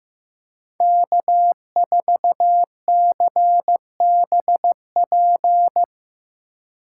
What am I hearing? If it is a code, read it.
Morse code, heard as a keyed tone, about 17 wpm, 700 Hz. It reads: K4CBP
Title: Speed Racing - USA 1x3 Call Signs 15wpm